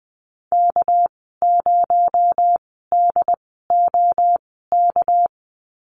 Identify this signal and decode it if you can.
Morse code, heard as a keyed tone, about 20 wpm, spaced as for 15 wpm, 700 Hz. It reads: K0DOK
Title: 238 Lesson 38g - Letter q - Call Signs 20wpm